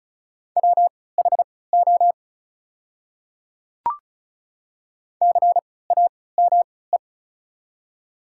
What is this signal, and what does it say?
Morse code, heard as a keyed tone, about 36 wpm, spaced as for 20 wpm, 700 Hz. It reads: WHO CAME
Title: 3x Character Spacing - Top 200 Words - 35wpm